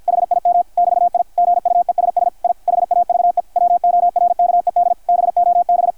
Morse code, heard as a keyed tone, about 50 wpm, 660 Hz, with radio noise. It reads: HIM <BT> I GUESS I HAVE WORKED BOB
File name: archived / CRSnet